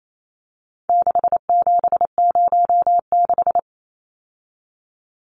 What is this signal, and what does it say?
Morse code, heard as a keyed tone, about 28 wpm, 700 Hz. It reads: 6706
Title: Numbers - 4-Digit 28wpm